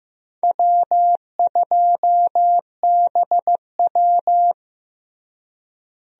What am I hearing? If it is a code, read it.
Morse code, heard as a keyed tone, about 15 wpm, 700 Hz. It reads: W2BW
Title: USA Call Signs - 4 Characters - Repeated 3x - 15wpm